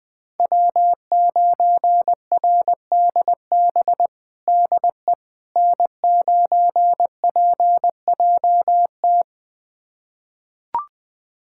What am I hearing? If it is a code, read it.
Morse code, heard as a keyed tone, about 20 wpm, 700 Hz. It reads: W9RDB DE N9PJT
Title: USA Callsign DE Callsign - 5 Characters - 20wpm